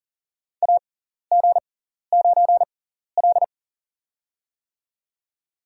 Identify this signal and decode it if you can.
Morse code, heard as a keyed tone, about 40 wpm, spaced as for 14 wpm, 700 Hz. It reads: AG9L
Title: USA Call Signs - 4 Characters - 6x Character Spacing - 40wpm